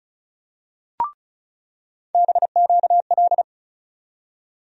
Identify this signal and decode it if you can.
Morse code, heard as a keyed tone, about 36 wpm, 700 Hz. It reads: BQL
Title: USA Call Sign Postfixes - 3 Letters 35wpm